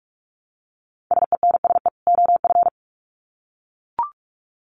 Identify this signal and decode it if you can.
Morse code, heard as a keyed tone, about 44 wpm, 700 Hz. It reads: SENSE OF
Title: Sets of 2 Words 45wpm